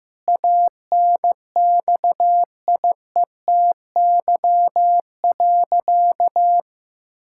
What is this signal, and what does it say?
Morse code, heard as a keyed tone, about 15 wpm, 700 Hz. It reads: ANXIETY.
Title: Sentences from Top 1600 Words 15wpm